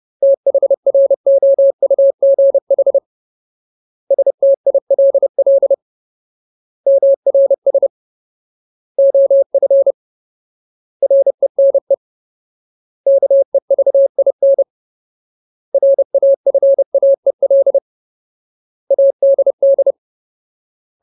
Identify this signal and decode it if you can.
Morse code, heard as a keyed tone, about 30 wpm, 550 Hz. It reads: THROUGH STILL MRS OF RENE KEVIN RAFAEL ADD
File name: Ultimate Word List / 30WPM Ultimate